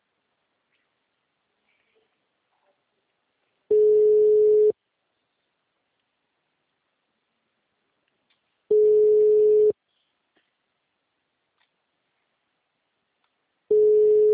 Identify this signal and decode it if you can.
Ringback tone (Europe, 425 Hz).